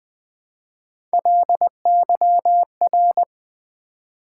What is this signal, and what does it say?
Morse code, heard as a keyed tone, about 20 wpm, 700 Hz. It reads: LYR
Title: Build A Word - 1 Letter At A Time - 5-Letter Words - 20wpm